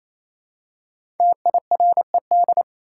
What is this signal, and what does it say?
Morse code, heard as a keyed tone, about 28 wpm, 700 Hz. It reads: TIRED
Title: Five Letter Words - 28wpm